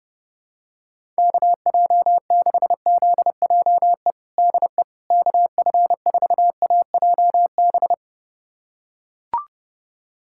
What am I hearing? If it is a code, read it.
Morse code, heard as a keyed tone, about 30 wpm, 700 Hz. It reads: KJ6ZJE DE KF4AJB
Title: USA Callsign DE Callsign 30wpm